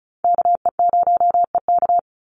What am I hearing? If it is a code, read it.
Morse code, heard as a keyed tone, about 34 wpm, 700 Hz. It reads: KE0EK
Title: Call Signs - 35wpm